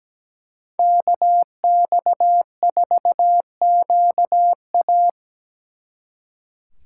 Morse code, heard as a keyed tone, about 20 wpm, 700 Hz. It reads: KX4QA
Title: Speed Racing - USA 2x2 Call Signs 17wpm